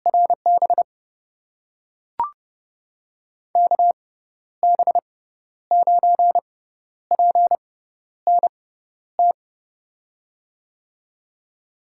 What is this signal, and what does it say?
Morse code, heard as a keyed tone, about 31 wpm, spaced as for 12 wpm, 700 Hz. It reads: RB KB9PNT
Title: USA Call Signs - 6 Characters - 6x Character Spacing - 30wpm